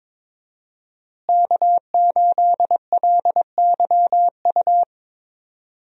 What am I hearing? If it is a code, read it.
Morse code, heard as a keyed tone, about 22 wpm, 700 Hz. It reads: K8LYU
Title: USA Call Signs - 5 Characters - Repeated 3x - 22wpm